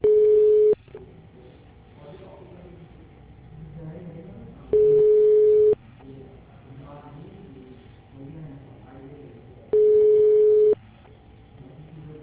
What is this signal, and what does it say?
Ringback tone (Europe, 425 Hz), with speech.